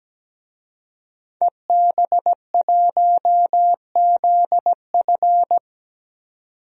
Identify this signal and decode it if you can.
Morse code, heard as a keyed tone, about 17 wpm, 700 Hz. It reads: EB1ZF
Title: DXCC-100 Call Signs - 5 Characters - 17wpm